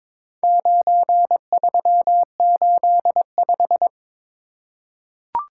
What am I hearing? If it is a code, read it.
Morse code, heard as a keyed tone, about 22 wpm, 700 Hz. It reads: 9385
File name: practice / 22wpm / Four Letter-Number Combinations - 22wpm